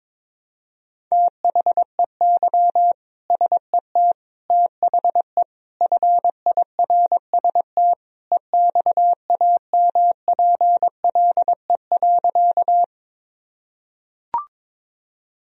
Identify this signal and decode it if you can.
Morse code, heard as a keyed tone, about 22 wpm, 700 Hz. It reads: THEY SET THE FIRST EXAMPLE.
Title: Sentences from Top 1000 Words - Review Part 1 - 22wpm